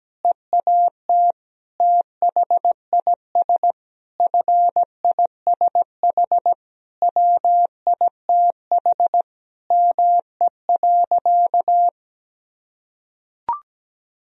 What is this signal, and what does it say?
Morse code, heard as a keyed tone, about 17 wpm, 700 Hz. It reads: EAT THIS FISH WITH ME.
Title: Sentences from Top 1000 Words - Review Part 1 - 17wpm